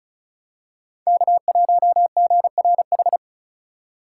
Morse code, heard as a keyed tone, about 36 wpm, 700 Hz. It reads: K1GRH
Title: USA Call Signs - 5 Characters - 4x Character Spacing - 35wpm